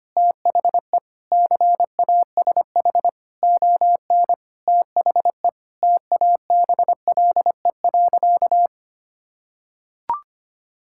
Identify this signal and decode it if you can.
Morse code, heard as a keyed tone, about 25 wpm, 700 Hz. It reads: THE CASH ON THE TABLE.
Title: Sentences from Top 1000 Words - Review Part 5 - 25wpm